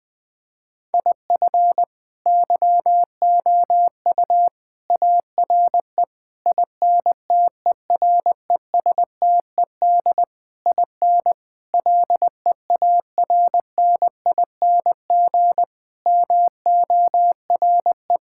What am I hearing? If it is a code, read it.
Morse code, heard as a keyed tone, about 20 wpm, 700 Hz. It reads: IF YOU ARE INTERESTED IN LEARNING MORE
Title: Sets of 8 Words - 20wpm